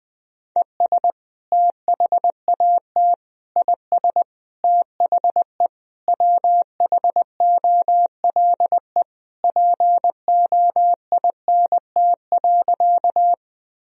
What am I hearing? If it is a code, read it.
Morse code, heard as a keyed tone, about 20 wpm, 700 Hz. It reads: ES THAT IS THE WHOLE POINT.
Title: Sentences from Top 400 Words and Abbreviations 20wpm